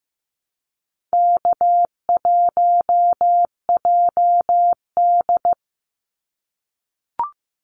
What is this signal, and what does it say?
Morse code, heard as a keyed tone, about 15 wpm, 700 Hz. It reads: K1JD
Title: NA SOTA Activator Callsigns 15wpm